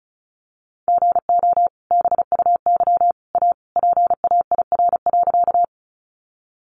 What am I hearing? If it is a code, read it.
Morse code, heard as a keyed tone, about 34 wpm, 700 Hz. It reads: GO BUY A PAIR.
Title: Sentences from Top 1200 Words 35wpm